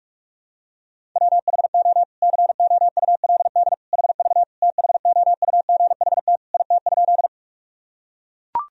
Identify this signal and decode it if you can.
Morse code, heard as a keyed tone, about 41 wpm, 700 Hz. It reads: WHO COULD HV THOUGHT IT?
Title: Sentences from Top 300 Words and Abbreviations 45wpm